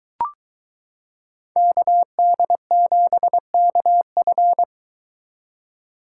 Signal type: Morse code

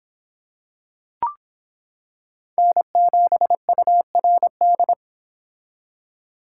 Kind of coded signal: Morse code

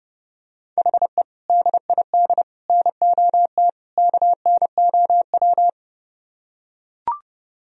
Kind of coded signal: Morse code

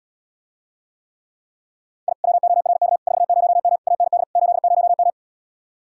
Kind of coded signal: Morse code

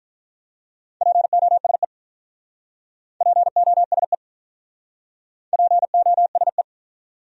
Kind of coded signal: Morse code

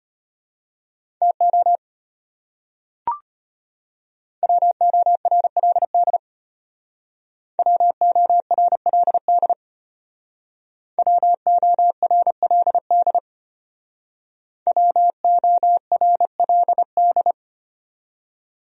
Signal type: Morse code

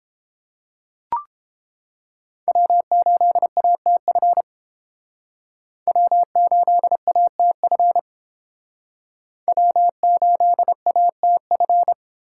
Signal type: Morse code

